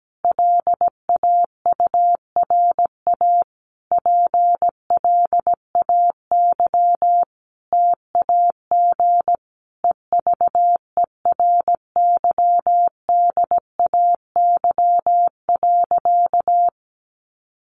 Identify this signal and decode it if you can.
Morse code, heard as a keyed tone, about 17 wpm, 700 Hz. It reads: LAURA PLAY TAG EVERYDAY.
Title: Additional Easy Phrases 17wpm